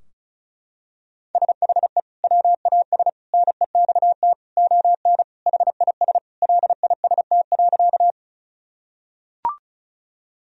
Morse code, heard as a keyed tone, about 36 wpm, 700 Hz. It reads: SHE WAS NEXT ON HIS LIST.
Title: Sentences from Top 1000 Words - Review Part 2 - 35wpm